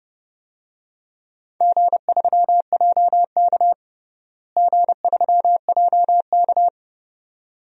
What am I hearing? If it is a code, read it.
Morse code, heard as a keyed tone, about 30 wpm, 700 Hz. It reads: G3JK G3JK
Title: DXCC-100 Call Signs - 4 Characters - Repeated 2x - 30wpm